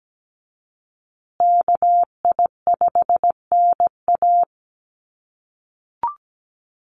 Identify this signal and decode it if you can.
Morse code, heard as a keyed tone, about 17 wpm, 700 Hz. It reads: KI5NA
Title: USA Call Signs Repeated 3x 17wpm